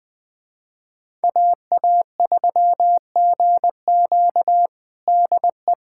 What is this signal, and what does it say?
Morse code, heard as a keyed tone, about 20 wpm, 700 Hz. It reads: AA3GQ DE
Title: USA Callsign DE Callsign - 5 Characters - 20wpm